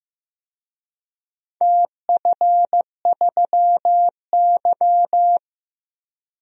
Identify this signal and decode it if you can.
Morse code, heard as a keyed tone, about 15 wpm, 700 Hz. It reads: TF3Y
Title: DXCC-100 Call Signs - 4 Characters - 15wpm